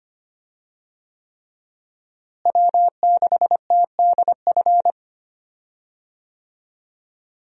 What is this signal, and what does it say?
Morse code, heard as a keyed tone, about 25 wpm, 700 Hz. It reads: W6TDF
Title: USA Call Signs - 5 Characters - Repeated 3x - 25wpm